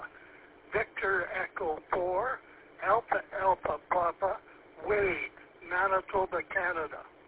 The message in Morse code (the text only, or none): none